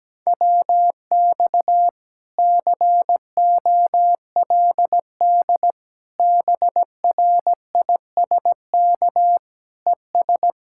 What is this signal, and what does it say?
Morse code, heard as a keyed tone, about 17 wpm, 700 Hz. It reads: WX COLD BRISK ES
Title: Speed-Racing Common QSO Phrases - 17wpm